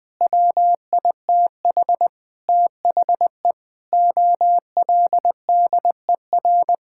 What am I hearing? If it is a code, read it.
Morse code, heard as a keyed tone, about 21 wpm, 700 Hz. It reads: WITH THE OLDER
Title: Sentences with 7-Word Phrases - Spoken Beginning - 20wpm